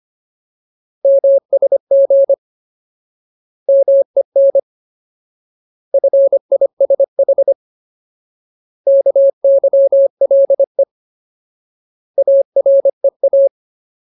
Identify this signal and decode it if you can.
Morse code, heard as a keyed tone, about 25 wpm, 550 Hz. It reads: MSG MEN FISH KYLE AREA
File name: Ultimate Word List / 25WPM Ultimate